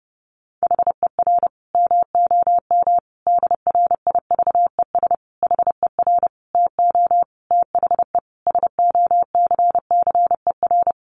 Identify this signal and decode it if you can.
Morse code, heard as a keyed tone, about 30 wpm, 700 Hz. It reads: HER MOM DRIVES HER TO THE SOCCER
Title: Easy Phrases 30wpm